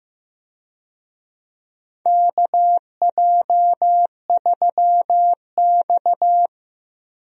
Morse code, heard as a keyed tone, about 16 wpm, 700 Hz. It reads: KJ3X
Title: USA Call Signs - 4 Characters - Repeated 2x - 15wpm